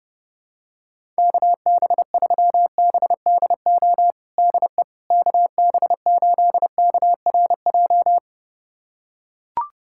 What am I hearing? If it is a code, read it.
Morse code, heard as a keyed tone, about 30 wpm, 700 Hz. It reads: KB3BDO DE KB8KRJ
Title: USA Callsign DE Callsign 30wpm